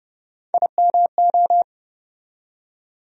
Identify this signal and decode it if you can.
Morse code, heard as a keyed tone, about 30 wpm, 700 Hz. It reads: IMO
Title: USA Call Sign Postfixes - 3 Letters 30wpm